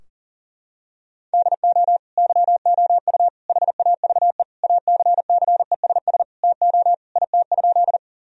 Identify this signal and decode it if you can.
Morse code, heard as a keyed tone, about 40 wpm, 700 Hz. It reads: DO YOU HAVE ACCESS TO IT?
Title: Sentences from Top 1000 Words - Review Part 4 - 40wpm